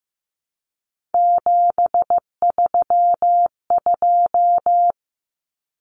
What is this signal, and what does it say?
Morse code, heard as a keyed tone, about 15 wpm, 700 Hz. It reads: 732
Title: Numbers - 3-Digit 15wpm